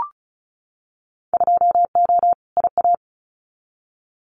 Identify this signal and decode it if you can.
Morse code, heard as a keyed tone, about 36 wpm, 700 Hz. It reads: 2O IA
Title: Winter Field Day Exchanges 35wpm